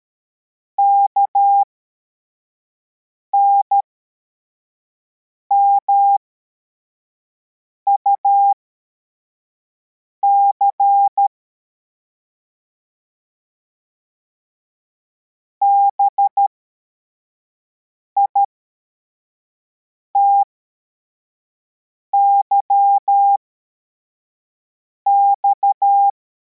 Morse code, heard as a keyed tone, about 13 wpm, spaced as for 4 wpm, 800 Hz. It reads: KNMUC BITYX